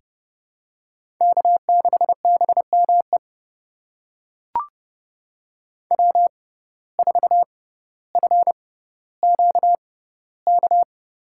Morse code, heard as a keyed tone, about 30 wpm, spaced as for 15 wpm, 700 Hz. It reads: K6BME W4FQK
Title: USA Call Signs - 5 Characters - 6x Character Spacing - 30wpm